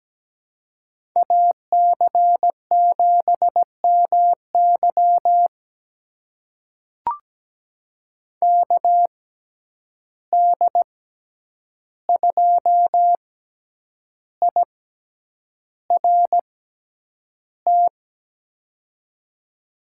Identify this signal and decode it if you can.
Morse code, heard as a keyed tone, about 17 wpm, spaced as for 8 wpm, 700 Hz. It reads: AC7MY KD2IRT
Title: USA Call Signs - 6x Character Spacing - 17wpm